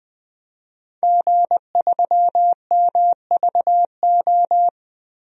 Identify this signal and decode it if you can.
Morse code, heard as a keyed tone, about 20 wpm, 700 Hz. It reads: G3MVO
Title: DXCC-100 Call Signs - 5 Characters - Repeated 2x - 20wpm